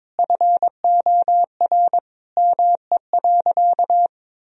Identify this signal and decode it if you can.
Morse code, heard as a keyed tone, about 22 wpm, 700 Hz. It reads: FOR ME.
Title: Sentences from Top 1000 Words - Review Part 4 - 22wpm